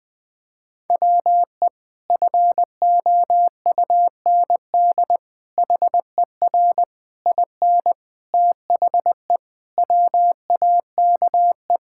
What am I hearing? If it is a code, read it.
Morse code, heard as a keyed tone, about 21 wpm, 700 Hz. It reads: WE FOUND HER IN THE WAKE
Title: Sentences with 4-Word Phrases - 20wpm